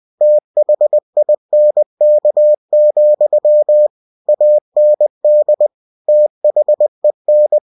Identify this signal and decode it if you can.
Morse code, heard as a keyed tone, about 20 wpm, 600 Hz. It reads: THINK, AND THEN
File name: cw-books / winnie-the-pooh / winnie-20wpm-page-0014